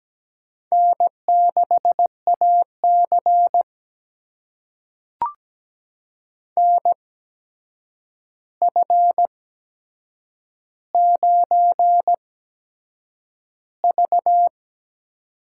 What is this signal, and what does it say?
Morse code, heard as a keyed tone, about 17 wpm, spaced as for 7 wpm, 700 Hz. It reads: N6AC NF9V
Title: USA Call Signs - 4 Characters - 8x Character Spacing - 17wpm